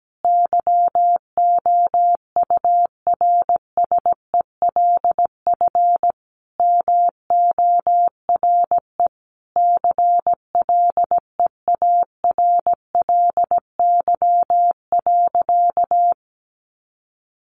Morse code, heard as a keyed tone, about 17 wpm, 700 Hz. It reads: YOURSELF MORE CLEARLY.
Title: Sentences from Top 1300 Words 17wpm